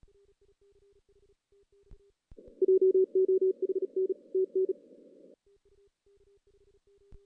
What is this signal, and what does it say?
Morse code, heard as a keyed tone, about 37 wpm, 380 Hz. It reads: JOHN TN
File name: 2016 JAN CW NAQP / 2016 Jan naqp k5ot - 0-5